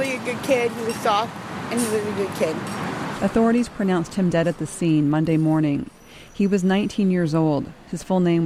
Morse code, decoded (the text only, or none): none